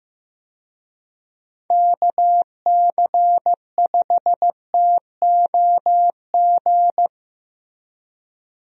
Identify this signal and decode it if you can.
Morse code, heard as a keyed tone, about 15 wpm, 700 Hz. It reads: KC5TOG
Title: USA Call Signs - 6 Characters - Repeated 2x - 15wpm